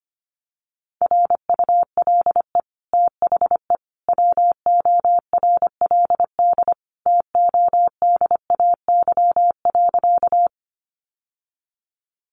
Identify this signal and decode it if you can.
Morse code, heard as a keyed tone, about 25 wpm, 700 Hz. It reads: RULE THE WORLD TODAY.
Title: Sentences from Top 600 Words 25wpm